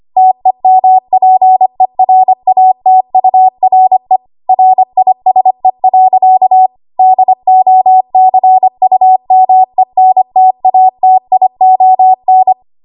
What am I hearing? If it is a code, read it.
Morse code, heard as a keyed tone, about 25 wpm, 750 Hz. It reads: TEMPERATURE RISE. DOCUMENTATION
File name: Archive / 25 WPM / 25WPM